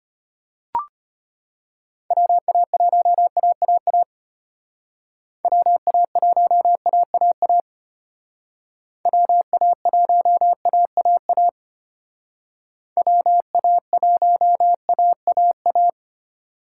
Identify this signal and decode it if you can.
Morse code, heard as a keyed tone, about 29 wpm, 700 Hz. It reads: WA1AAA WA1AAA WA1AAA WA1AAA
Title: Speed Racing - USA 2x3 Call Signs 25wpm